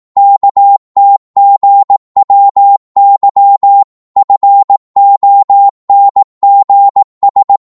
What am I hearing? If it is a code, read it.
Morse code, heard as a keyed tone, about 18 wpm, 800 Hz. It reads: KTGWY FONGS
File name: Morse / ltr-18wpm